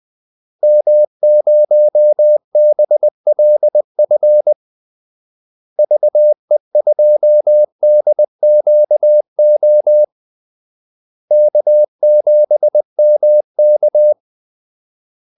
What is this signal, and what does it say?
Morse code, heard as a keyed tone, about 20 wpm, 600 Hz. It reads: M0BLF VE2DQO K7MK
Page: Kaldesignaler 20 wpm